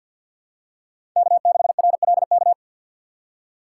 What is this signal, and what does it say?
Morse code, heard as a keyed tone, about 53 wpm, 700 Hz. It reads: K6RLK
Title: USA Call Signs - 5 Characters - 6x Character Spacing - 50wpm